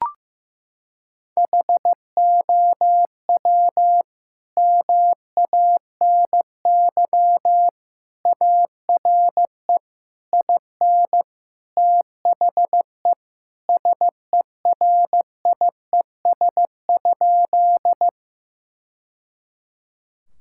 Morse code, heard as a keyed tone, about 15 wpm, 700 Hz. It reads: HOW MANY ARE IN THE SERIES?
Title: Sentences from Top 1000 Words - Review Part 5 - 15wpm